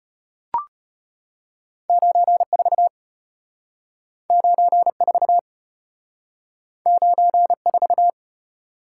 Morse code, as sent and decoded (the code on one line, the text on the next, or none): ----. ....- / ----. ....- / ----. ....-
94 94 94